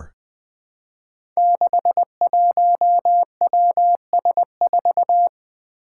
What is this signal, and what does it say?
Morse code, heard as a keyed tone, about 20 wpm, 700 Hz. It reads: 61WS4
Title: Five Challenging Letter-Number Combinations 20wpm